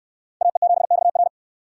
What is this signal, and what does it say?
Morse code, heard as a keyed tone, about 67 wpm, 700 Hz. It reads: AE7XR